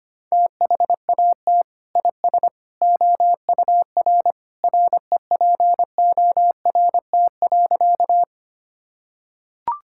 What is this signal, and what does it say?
Morse code, heard as a keyed tone, about 25 wpm, 700 Hz. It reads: THAT IS OUR REPORT.
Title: Sentences from Top 1000 Words - Review Part 6 - 25wpm